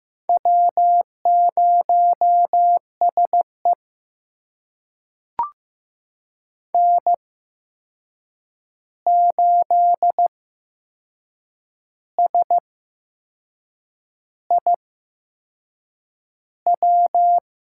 Morse code, heard as a keyed tone, about 15 wpm, spaced as for 6 wpm, 700 Hz. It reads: W0SE N8SIW
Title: USA Call Signs - 8x Character Spacing - 15wpm